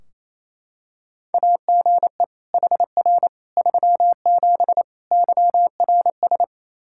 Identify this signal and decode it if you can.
Morse code, heard as a keyed tone, about 28 wpm, 700 Hz. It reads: AGE HR 37 YRS
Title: Practice QSO - Exchange 3 - More Info - 28wpm